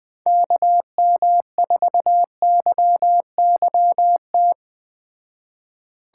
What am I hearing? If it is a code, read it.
Morse code, heard as a keyed tone, about 19 wpm, 700 Hz. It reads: KM4YYT
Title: Speed Racing - USA 2x3 Call Signs 17wpm